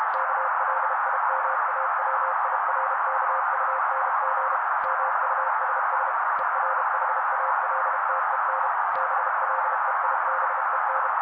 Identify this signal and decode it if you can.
Morse code, heard as a keyed tone, about 32 wpm, 530 Hz. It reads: KLIMAWIRKUNG MUSS ERHALTEN BLEIBEN
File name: daily / 31wpm noise